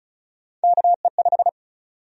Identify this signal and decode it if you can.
Morse code, heard as a keyed tone, about 34 wpm, 700 Hz. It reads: KE5
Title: USA Call Sign Prefixes - 35wpm